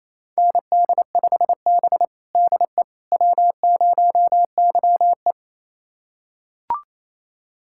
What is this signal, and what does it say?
Morse code, heard as a keyed tone, about 28 wpm, 700 Hz. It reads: ND5B DE W0YE
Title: USA Callsign DE Callsign - 4 Characters - 28wpm